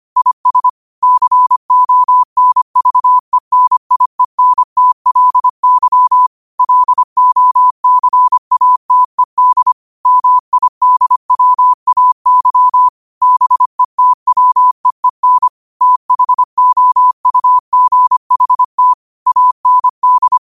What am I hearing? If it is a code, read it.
Morse code, heard as a keyed tone, about 25 wpm, 1000 Hz. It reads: IS CONVENIENTLY LOCATED MIDWAY BETWEEN THOUGHT AND
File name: 25wpm